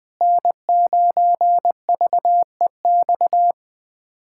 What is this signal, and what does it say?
Morse code, heard as a keyed tone, about 20 wpm, 700 Hz. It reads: N9VEX
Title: USA Callsign DE Callsign - 5 Characters - 20wpm